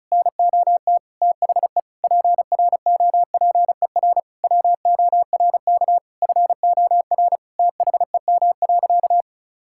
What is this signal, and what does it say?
Morse code, heard as a keyed tone, about 36 wpm, 700 Hz. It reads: NOT THE PROPER WORK FOR THEM.
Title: Sentences from Top 1000 Words - Review Part 5 - 35wpm